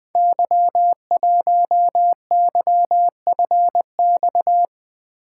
Morse code, heard as a keyed tone, about 20 wpm, 700 Hz. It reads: Y1YFX
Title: Code Groups - 5 Letter:Numbers - 20wpm